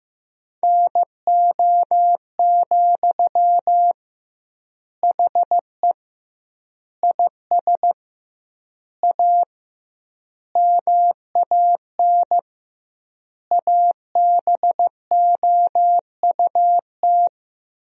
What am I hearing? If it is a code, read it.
Morse code, heard as a keyed tone, about 15 wpm, 700 Hz. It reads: NO, HE IS A MAN ABOUT
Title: 2x Word-Spacing - Sentences from Top 400 Words - 15wpm